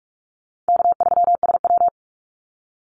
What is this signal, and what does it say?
Morse code, heard as a keyed tone, about 44 wpm, 700 Hz. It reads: K3SW
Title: Easy Call Signs - 45wpm